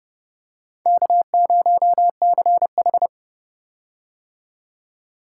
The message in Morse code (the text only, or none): K0CH